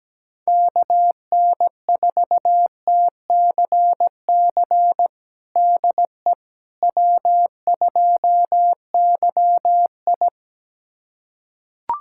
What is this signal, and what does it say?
Morse code, heard as a keyed tone, about 17 wpm, 700 Hz. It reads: KN4TCC DE W2YI
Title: USA Callsign DE Callsign 17wpm